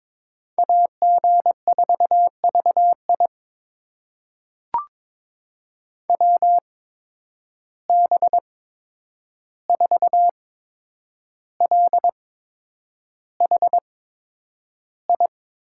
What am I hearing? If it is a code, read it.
Morse code, heard as a keyed tone, about 21 wpm, spaced as for 9 wpm, 700 Hz. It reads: AG4VI WB4LHI
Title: USA Call Signs - 8x Character Spacing - 22wpm